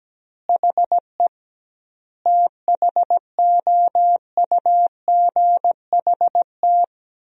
Morse code, heard as a keyed tone, about 17 wpm, 700 Hz. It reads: HE THOUGHT
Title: 2x Word-Spacing - Sentences from Top 300 Words - 17wpm